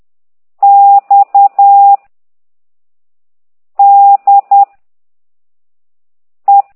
Morse code, heard as a keyed tone, about 10 wpm, 800 Hz. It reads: X D E
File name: exerc7